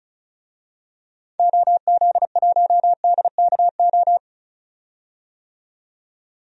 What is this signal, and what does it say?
Morse code, heard as a keyed tone, about 36 wpm, 700 Hz. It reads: OZ1DKO
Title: DXCC-100 Call Signs - 6 Characters - 35wpm